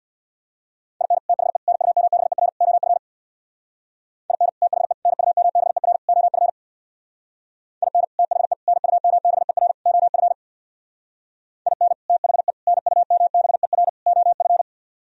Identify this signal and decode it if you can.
Morse code, heard as a keyed tone, about 60 wpm, 700 Hz. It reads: IN THE NUMBER OF IN THE NUMBER OF IN THE NUMBER OF IN THE NUMBER OF
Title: Speed-Racing - Sets of 4 Words 50wpm